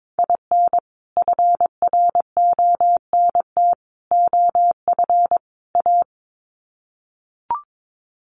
Morse code, heard as a keyed tone, about 22 wpm, 700 Hz. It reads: IN FRONT OF A
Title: Sets of 4 Words 22wpm